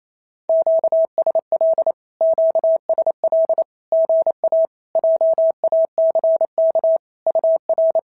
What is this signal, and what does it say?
Morse code, heard as a keyed tone, about 28 wpm, 650 Hz. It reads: QSL QSL GA JACK UR
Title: SOTA QSO Practice 28wpm